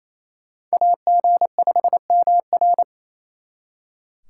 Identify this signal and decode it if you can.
Morse code, heard as a keyed tone, about 34 wpm, 700 Hz. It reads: AG5MR
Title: Speed Racing - USA 2x2 Call Signs 28wpm